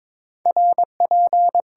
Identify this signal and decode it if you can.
Morse code, heard as a keyed tone, about 22 wpm, 700 Hz. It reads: RP
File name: Top 90 QSO Elements - 6x Character Spacing - 22wpm